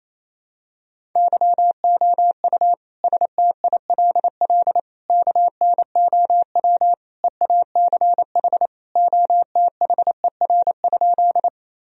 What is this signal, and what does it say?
Morse code, heard as a keyed tone, about 28 wpm, 700 Hz. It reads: YOU STILL KNOW EACH OTHER?
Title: Sentences from Top 1000 Words - Review Part 1 - 28wpm